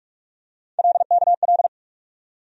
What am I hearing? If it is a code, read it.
Morse code, heard as a keyed tone, about 45 wpm, 700 Hz. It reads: LKL
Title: USA Call Sign Postfixes - 3 Letters 45wpm